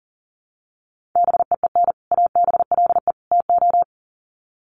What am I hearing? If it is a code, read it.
Morse code, heard as a keyed tone, about 38 wpm, 700 Hz. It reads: BEEN ABLE TO
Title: Sets of 3 Words 40wpm